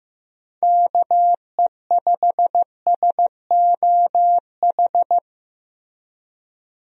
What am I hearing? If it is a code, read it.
Morse code, heard as a keyed tone, about 15 wpm, 700 Hz. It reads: KE5SOH
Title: USA Call Signs - 6 Characters - Repeated 3x - 15wpm